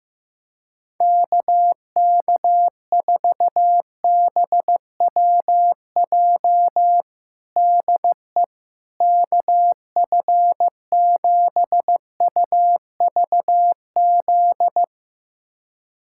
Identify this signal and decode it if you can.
Morse code, heard as a keyed tone, about 16 wpm, 700 Hz. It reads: KK4BWJ DE KF7UVZ
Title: USA Callsign DE Callsign - 6 Characters - 15wpm